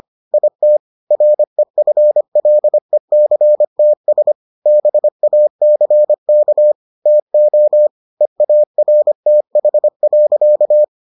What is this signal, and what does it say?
Morse code, heard as a keyed tone, about 25 wpm, 600 Hz. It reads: IT REFLECTS BACK TO EARTH.